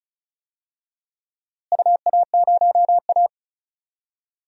Morse code, heard as a keyed tone, about 34 wpm, 700 Hz. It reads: UA0A
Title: DXCC-100 Call Signs - 4 Characters - Repeated 3x - 35wpm